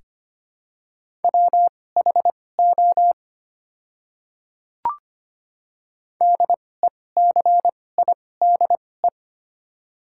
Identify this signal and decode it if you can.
Morse code, heard as a keyed tone, about 25 wpm, spaced as for 18 wpm, 700 Hz. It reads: WHO DECIDE
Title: 2x Character Spacing - Top 500 Words - 25wpm